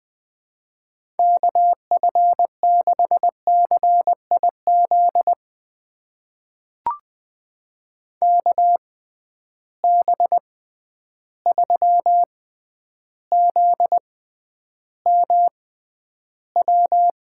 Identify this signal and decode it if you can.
Morse code, heard as a keyed tone, about 20 wpm, spaced as for 10 wpm, 700 Hz. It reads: KF6CIZ KB3ZMW
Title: USA Call Signs - 6 Characters - 6x Character Spacing - 20wpm